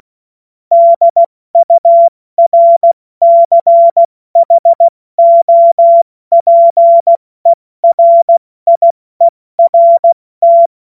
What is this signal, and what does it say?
Morse code, heard as a keyed tone, about 16 wpm, 680 Hz. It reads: DURCHOPERIERT
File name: Morsekurs / Witze / Witz 02 April 2024 70 F80